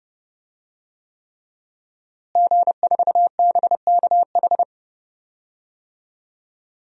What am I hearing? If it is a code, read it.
Morse code, heard as a keyed tone, about 30 wpm, 700 Hz. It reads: G4BKH
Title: DXCC-100 Call Signs - 5 Characters - Repeated 3x - 30wpm